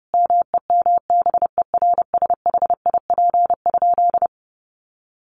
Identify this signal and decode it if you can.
Morse code, heard as a keyed tone, about 30 wpm, 700 Hz. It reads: MEMBERSHIP?
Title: Sentences from Top 1400 Words 30wpm